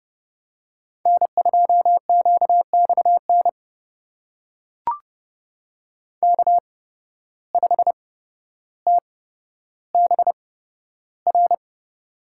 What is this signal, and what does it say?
Morse code, heard as a keyed tone, about 29 wpm, spaced as for 13 wpm, 700 Hz. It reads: N2QXN K5TBR
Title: USA Call Signs - 5 Characters - 8x Character Spacing - 30wpm